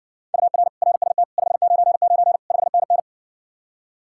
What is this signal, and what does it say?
Morse code, heard as a keyed tone, about 60 wpm, 700 Hz. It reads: UR RST 599 5NN